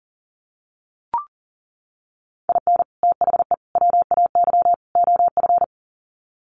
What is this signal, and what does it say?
Morse code, heard as a keyed tone, about 40 wpm, 700 Hz. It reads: IN THE WAY OF
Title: Sets of 4 Words 40wpm